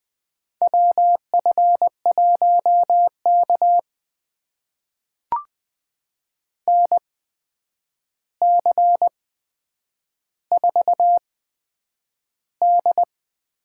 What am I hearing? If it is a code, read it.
Morse code, heard as a keyed tone, about 20 wpm, spaced as for 9 wpm, 700 Hz. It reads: WF1K NC4D
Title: USA Call Signs - 4 Characters - 8x Character Spacing - 20wpm